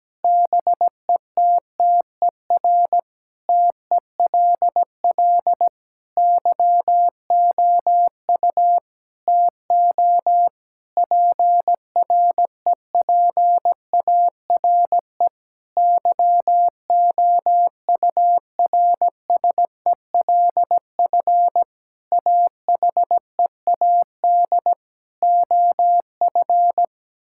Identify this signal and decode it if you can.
Morse code, heard as a keyed tone, about 17 wpm, 700 Hz. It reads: BETTER TELL YOU TO PREPARE YOURSELF AHEAD OF
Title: Sentences from Top 1000 Words - Review Part 6 - 17wpm